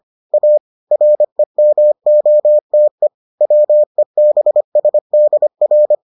Morse code, heard as a keyed tone, about 25 wpm, 600 Hz. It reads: A REMOTE WEBSDR